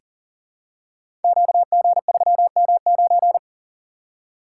Morse code, heard as a keyed tone, about 40 wpm, 690 Hz. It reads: QG3M9
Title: Code Groups - 5 Letter:Numbers - 40wpm